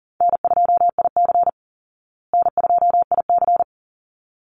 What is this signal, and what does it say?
Morse code, heard as a keyed tone, about 40 wpm, 700 Hz. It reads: N2IC N2IC
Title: Call Signs Repeated 3x - 40wpm